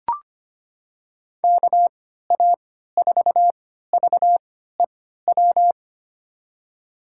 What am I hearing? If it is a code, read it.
Morse code, heard as a keyed tone, about 25 wpm, spaced as for 14 wpm, 700 Hz. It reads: KA4VEW
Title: USA Call Signs - 3x Character Spacing - 25wpm